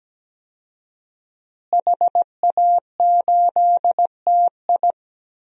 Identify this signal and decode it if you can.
Morse code, heard as a keyed tone, about 17 wpm, 700 Hz. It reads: HA8TI
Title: DXCC-100 Call Signs - 5 Characters - Repeated 3x - 17wpm